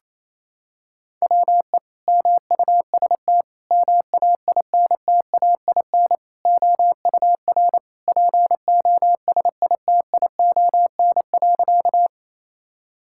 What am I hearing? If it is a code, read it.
Morse code, heard as a keyed tone, about 28 wpm, 700 Hz. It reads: WE MUST MAINTAIN OUR POSITION.
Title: Sentences from Top 1000 Words - Review Part 5 - 28wpm